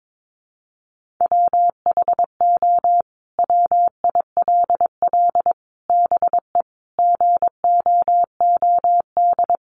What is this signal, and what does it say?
Morse code, heard as a keyed tone, about 22 wpm, 700 Hz. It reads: WHO WILL BE GOOD
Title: Sentences from Top 1100 Words 22wpm